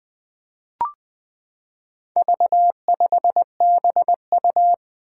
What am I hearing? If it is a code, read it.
Morse code, heard as a keyed tone, about 20 wpm, 700 Hz. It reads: V5BU
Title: Four Challenging Letter-Number Combinations 20wpm